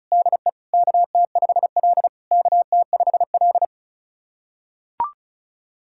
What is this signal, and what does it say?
Morse code, heard as a keyed tone, about 36 wpm, 690 Hz. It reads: DE KT5L KT5L
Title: SOTA QSO Practice 35wpm